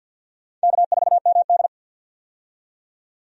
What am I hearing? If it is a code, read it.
Morse code, heard as a keyed tone, about 49 wpm, 700 Hz. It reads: K4MD
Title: USA Call Signs - 4 Characters - 50wpm